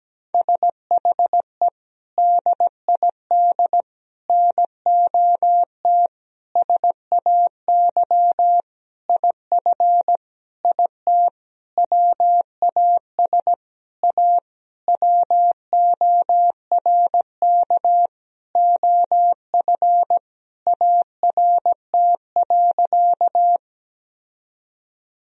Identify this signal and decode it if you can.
Morse code, heard as a keyed tone, about 17 wpm, 700 Hz. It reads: SHE DID NOT SAY IF IT WAS A WORK OF ART.
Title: Sentences from Top 1000 Words - Review Part 3 - 17wpm